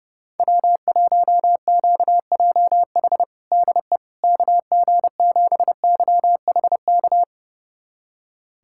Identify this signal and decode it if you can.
Morse code, heard as a keyed tone, about 32 wpm, 700 Hz. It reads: W1QJH DE KG7YHK
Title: USA Callsign DE Callsign 30wpm